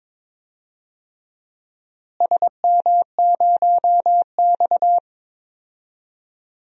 Morse code, heard as a keyed tone, about 22 wpm, 700 Hz. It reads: SM0X
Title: DXCC-100 Call Signs - 4 Characters - Repeated 2x - 22wpm